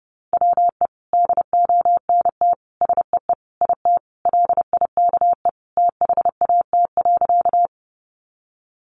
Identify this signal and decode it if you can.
Morse code, heard as a keyed tone, about 30 wpm, 700 Hz. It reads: WE DONT SEE IT LIKE THAT.
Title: Sentences from Top 300 Words 30wpm